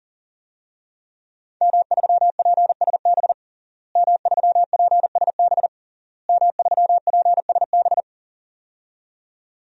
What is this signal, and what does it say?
Morse code, heard as a keyed tone, about 40 wpm, 700 Hz. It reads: M3PSB M3PSB M3PSB
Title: DXCC-100 Call Signs - 5 Characters - Repeated 3x - 40wpm